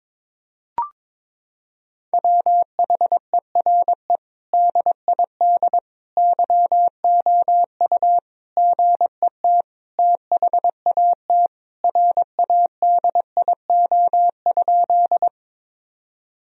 Morse code, heard as a keyed tone, about 22 wpm, 700 Hz. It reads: WHERE DID YOU GET THAT RADIO?
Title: Sentences from Top 1000 Words - Review Part 3 - 22wpm